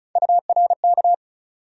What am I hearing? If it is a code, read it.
Morse code, heard as a keyed tone, about 34 wpm, 700 Hz. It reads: URK
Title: USA Call Sign Postfixes - 3 Letters 35wpm